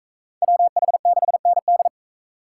Morse code, heard as a keyed tone, about 37 wpm, 700 Hz. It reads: WH6ND